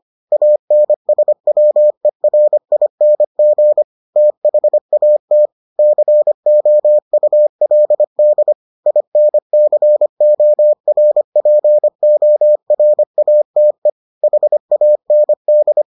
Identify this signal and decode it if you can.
Morse code, heard as a keyed tone, about 25 wpm, 600 Hz. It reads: ANSWERING THAT COULD INCORPORATE HAND